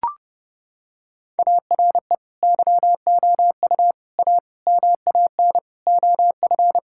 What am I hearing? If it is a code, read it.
Morse code, heard as a keyed tone, about 30 wpm, 700 Hz. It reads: ARE YOU A MAN OF
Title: Sentences from Top 1000 Words - Review Part 6 - 30wpm